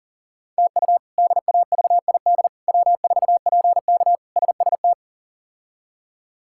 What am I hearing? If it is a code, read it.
Morse code, heard as a keyed tone, about 40 wpm, 700 Hz. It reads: TU DAVID W4PK SST
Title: K1USN SST QSO Practice - 40wpm